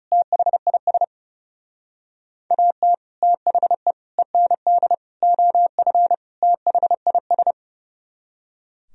Morse code, heard as a keyed tone, about 40 wpm, 700 Hz. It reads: THIS AT THE END OF THIS